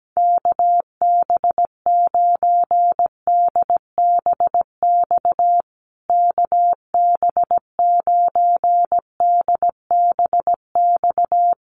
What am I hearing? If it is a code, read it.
Morse code, heard as a keyed tone, about 17 wpm, 700 Hz. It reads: KB9DBX KB9DBX
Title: USA Call Signs Repeated 3x 17wpm